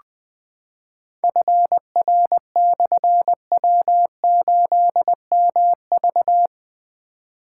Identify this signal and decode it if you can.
Morse code, heard as a keyed tone, about 20 wpm, 700 Hz. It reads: FR/W8MV
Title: Difficult Call Signs 20wpm